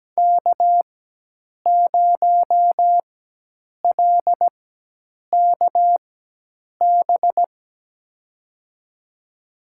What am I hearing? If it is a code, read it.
Morse code, heard as a keyed tone, about 17 wpm, spaced as for 8 wpm, 700 Hz. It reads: K0LKB
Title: USA Call Signs - 4x Character Spacing - 17wpm